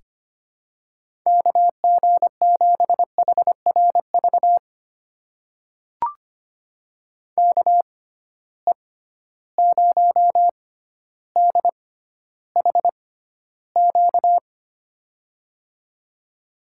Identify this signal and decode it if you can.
Morse code, heard as a keyed tone, about 25 wpm, spaced as for 13 wpm, 700 Hz. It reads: KG7HRV KE0DHQ
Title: USA Call Signs - 6 Characters - 6x Character Spacing - 25wpm